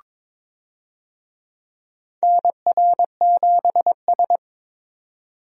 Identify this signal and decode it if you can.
Morse code, heard as a keyed tone, about 23 wpm, 700 Hz. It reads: NR7S
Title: USA Call Signs - 4 Characters - Repeated 3x - 22wpm